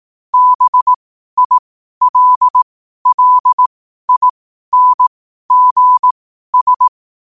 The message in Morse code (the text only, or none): BILLINGS